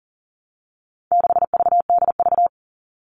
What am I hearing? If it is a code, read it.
Morse code, heard as a keyed tone, about 40 wpm, 700 Hz. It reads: 6VDV
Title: All Cut Numbers - 4-Digit 40wpm